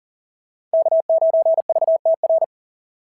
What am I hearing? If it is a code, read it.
Morse code, heard as a keyed tone, about 40 wpm, 690 Hz. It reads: K9VTR
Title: SOTA QSO Practice 40wpm